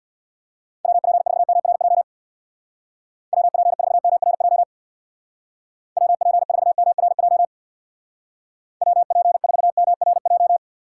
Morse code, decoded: WP4MRJ WP4MRJ WP4MRJ WP4MRJ

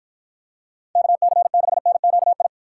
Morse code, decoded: KC6NQI

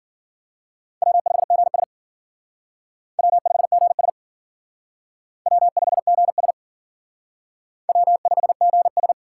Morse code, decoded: W5GS W5GS W5GS W5GS

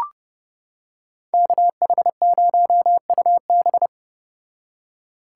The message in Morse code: KH0UB

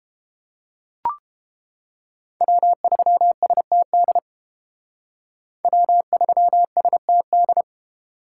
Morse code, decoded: W3STD W3STD

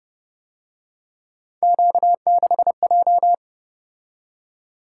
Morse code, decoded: Q6J

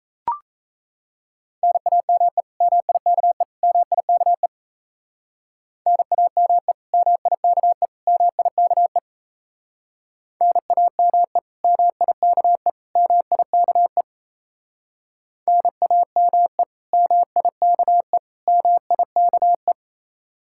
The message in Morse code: NAME MIKE MIKE NAME MIKE MIKE NAME MIKE MIKE NAME MIKE MIKE

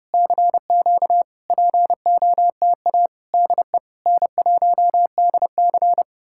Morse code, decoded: CQ POTA DE N1DC

